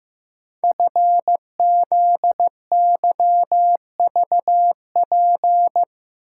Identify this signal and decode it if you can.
Morse code, heard as a keyed tone, about 15 wpm, 700 Hz. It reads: FZYVP
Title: Code Groups - 5 Letter:Numbers - 15wpm